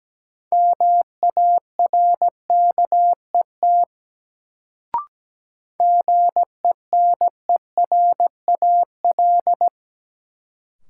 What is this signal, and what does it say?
Morse code, heard as a keyed tone, about 17 wpm, 700 Hz. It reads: MARKET GENERAL
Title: Sentences from Top 600 Words - New Word Drill - 17wpm